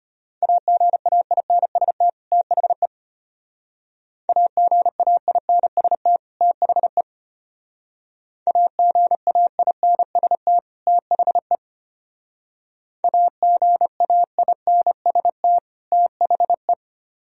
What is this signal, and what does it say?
Morse code, heard as a keyed tone, about 32 wpm, 700 Hz. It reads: AGAINST THE AGAINST THE AGAINST THE AGAINST THE